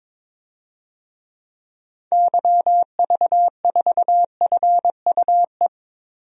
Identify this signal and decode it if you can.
Morse code, heard as a keyed tone, about 22 wpm, 700 Hz. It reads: YV4FUE
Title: DXCC-100 Call Signs - 6 Characters - Repeated 2x - 22wpm